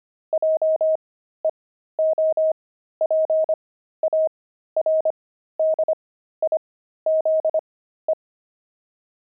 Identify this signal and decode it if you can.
Morse code, heard as a keyed tone, about 25 wpm, spaced as for 13 wpm, 630 Hz. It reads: JEOPARDIZE
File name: pangram-fs-13-cs-25-enemy